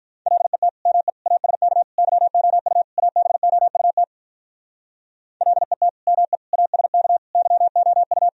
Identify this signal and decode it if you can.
Morse code, heard as a keyed tone, about 43 wpm, 700 Hz. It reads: LET ME ASK YOU ABOUT LET ME ASK YOU
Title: Speed-Racing - Sets of 5 Words 35wpm